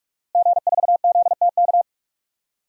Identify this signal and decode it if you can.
Morse code, heard as a keyed tone, about 44 wpm, 700 Hz. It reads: G4ZTK
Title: DXCC-100 Call Signs - 5 Characters - Repeated 2x - 45wpm